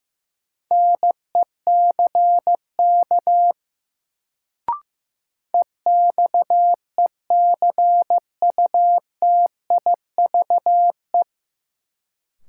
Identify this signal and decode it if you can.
Morse code, heard as a keyed tone, about 15 wpm, 700 Hz. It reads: NECK EXECUTIVE
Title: Sets of 5 Words Encore - New Word Drill - 15wpm